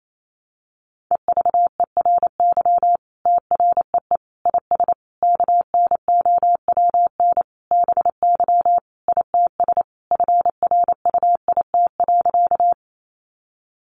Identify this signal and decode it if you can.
Morse code, heard as a keyed tone, about 28 wpm, 700 Hz. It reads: EVERY TREE IS KNOWN BY ITS FRUIT.
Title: Sentences from Top 1300 Words 28wpm